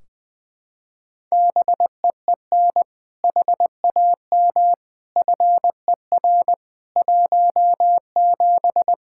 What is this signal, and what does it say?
Morse code, heard as a keyed tone, about 20 wpm, 700 Hz. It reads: BEEN HAM FER 17
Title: Practice QSO - Exchange 3 - More Info - 20wpm